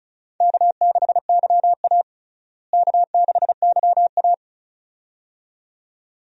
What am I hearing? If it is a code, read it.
Morse code, heard as a keyed tone, about 34 wpm, 700 Hz. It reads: K6YA K6YA
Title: USA Call Signs - 4 Characters - Repeated 2x - 35wpm